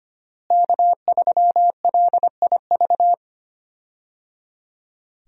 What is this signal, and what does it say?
Morse code, heard as a keyed tone, about 25 wpm, 700 Hz. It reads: K3LIV
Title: USA Call Signs - 5 Characters - Repeated 3x - 25wpm